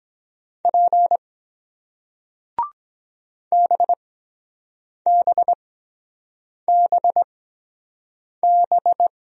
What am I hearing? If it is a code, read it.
Morse code, heard as a keyed tone, about 23 wpm, spaced as for 7 wpm, 700 Hz. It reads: P BBBB